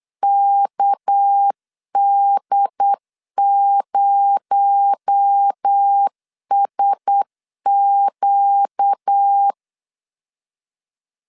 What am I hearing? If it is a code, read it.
Morse code, heard as a keyed tone, about 8 wpm, 800 Hz. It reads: KD0SQ
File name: CW / DET / 1 6 1500C S B 8wpm 5142